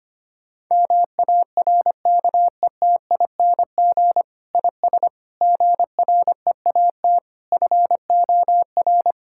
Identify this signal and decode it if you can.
Morse code, heard as a keyed tone, about 25 wpm, 700 Hz. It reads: MARKETING IS GREAT FOR
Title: Sentences from Top 1000 Words - Review Part 5 - 25wpm